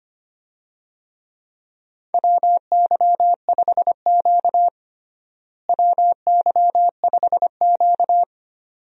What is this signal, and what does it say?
Morse code, heard as a keyed tone, about 25 wpm, 700 Hz. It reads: WY5Q WY5Q
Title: USA Call Signs - 4 Characters - Repeated 2x - 25wpm